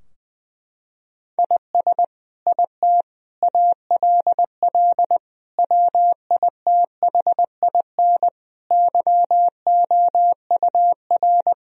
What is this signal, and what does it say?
Morse code, heard as a keyed tone, about 20 wpm, 700 Hz. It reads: IS IT ALL WITHIN YOUR
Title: Sentences from Top 1000 Words - Review Part 2 - 20wpm